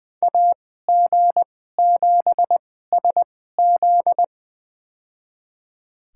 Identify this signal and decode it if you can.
Morse code, heard as a keyed tone, about 20 wpm, spaced as for 14 wpm, 700 Hz. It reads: AG7SZ
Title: USA Call Signs - 5 Characters - 2x Character Spacing - 20wpm